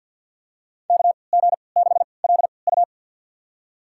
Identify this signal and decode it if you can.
Morse code, heard as a keyed tone, about 53 wpm, spaced as for 29 wpm, 700 Hz. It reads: KG6LU
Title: USA Call Signs - 5 Characters - 3x Character Spacing - 50wpm